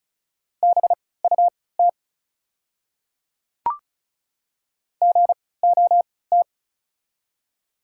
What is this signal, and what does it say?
Morse code, heard as a keyed tone, about 36 wpm, spaced as for 20 wpm, 700 Hz. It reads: BUT GOT
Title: 3x Character Spacing - Top 300 Words - 35wpm